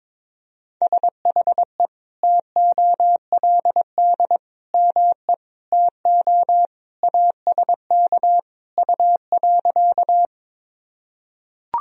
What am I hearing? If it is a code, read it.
Morse code, heard as a keyed tone, about 22 wpm, 700 Hz. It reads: SHE TOLD ME TO ASK U.
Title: Sentences from Top 400 Words and Abbreviations 22wpm